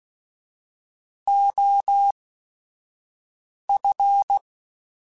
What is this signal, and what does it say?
Morse code, heard as a keyed tone, about 16 wpm, spaced as for 5 wpm, 800 Hz. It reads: OF
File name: Technical Subjects / Technical Subjects-mp3s / OR Gate @5